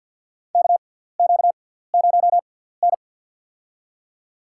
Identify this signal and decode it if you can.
Morse code, heard as a keyed tone, about 53 wpm, spaced as for 17 wpm, 700 Hz. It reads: KQ0N
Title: USA Call Signs - 6x Character Spacing - 50wpm